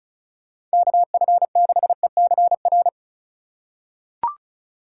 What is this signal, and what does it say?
Morse code, heard as a keyed tone, about 34 wpm, 700 Hz. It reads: KF6ECR
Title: USA Call Signs - 4x Character Spacing - 35wpm